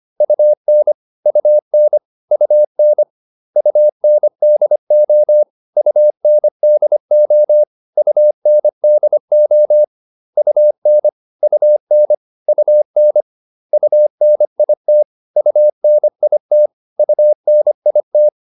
Morse code, heard as a keyed tone, about 25 wpm, 600 Hz. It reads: UN UN UN UNDO UNDO UNDO UN UN UN UNIT UNIT UNIT
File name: UN4 25